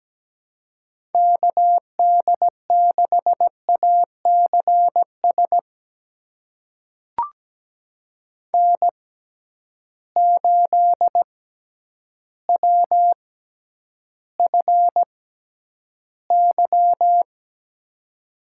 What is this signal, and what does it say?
Morse code, heard as a keyed tone, about 17 wpm, spaced as for 9 wpm, 700 Hz. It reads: KD6ACS N8WFY
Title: USA Call Signs - 6x Character Spacing - 17wpm